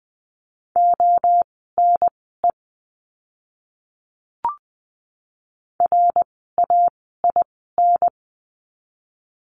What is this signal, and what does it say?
Morse code, heard as a keyed tone, about 20 wpm, spaced as for 14 wpm, 700 Hz. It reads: ONE RAIN
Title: 068 Lesson 12f - Letter d - Words 20wpm